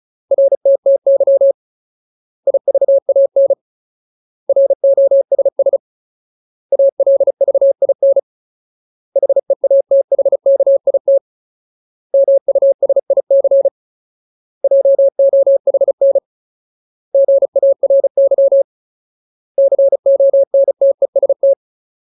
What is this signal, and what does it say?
Morse code, heard as a keyed tone, about 34 wpm, 550 Hz. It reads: RTTY IVAN ROSS ALVIN HEATHKIT MUSIC JOHN GARY CONTEST
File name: Ultimate Word List / 35WPM Ultimate